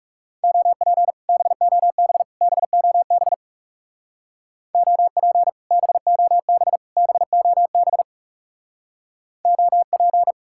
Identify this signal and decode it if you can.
Morse code, heard as a keyed tone, about 36 wpm, 700 Hz. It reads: OP BOB BOB OP BOB BOB OP
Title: Speed-Racing Common QSO Phrases Encore - 30wpm